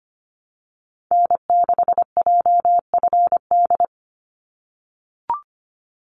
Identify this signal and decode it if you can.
Morse code, heard as a keyed tone, about 25 wpm, 700 Hz. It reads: N6JFD
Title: NA SOTA Activator Callsigns 25wpm